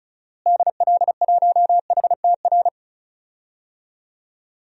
Morse code, heard as a keyed tone, about 36 wpm, 700 Hz. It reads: DL1HTR
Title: DXCC-100 Call Signs - 6 Characters - Repeated 2x - 35wpm